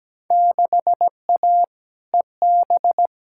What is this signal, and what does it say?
Morse code, heard as a keyed tone, about 17 wpm, 700 Hz. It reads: 6A EB
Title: Field Day Exchanges Encore - 17wpm